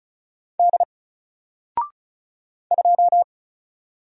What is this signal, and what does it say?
Morse code, heard as a keyed tone, about 35 wpm, 700 Hz. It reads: D 2
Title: Single Letter-Number Alt 35wpm